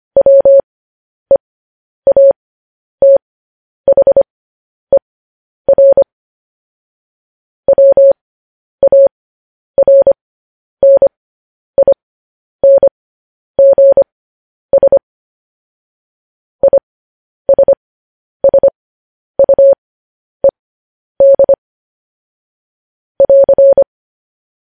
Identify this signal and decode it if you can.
Morse code, heard as a keyed tone, about 25 wpm, spaced as for 10 wpm, 550 Hz. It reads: WEATHER WARNINGS ISSUED <AR>
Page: News Headlines in Morse Code at 10 WPM